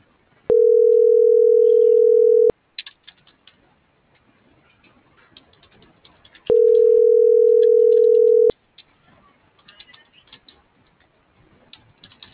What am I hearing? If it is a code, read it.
Ringback tone (North America), 440 + 480 Hz.